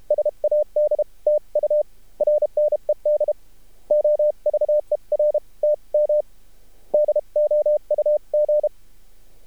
Morse code, heard as a keyed tone, about 32 wpm, spaced as for 27 wpm, 600 Hz. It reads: SAD TURNED OVER TM DOUG
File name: archived / CRSnet